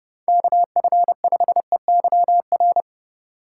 Morse code, heard as a keyed tone, about 32 wpm, 700 Hz. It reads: KF5EYR